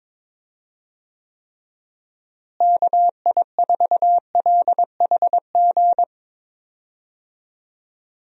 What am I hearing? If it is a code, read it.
Morse code, heard as a keyed tone, about 22 wpm, 700 Hz. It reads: KI4LHG
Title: USA Call Signs - 6 Characters - Repeated 2x - 22wpm